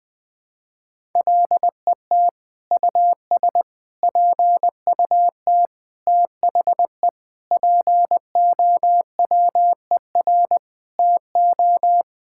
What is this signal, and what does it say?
Morse code, heard as a keyed tone, about 20 wpm, 700 Hz. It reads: LET US PUT THE POWER TO
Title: Sentences from Top 1000 Words - Review Part 5 - 20wpm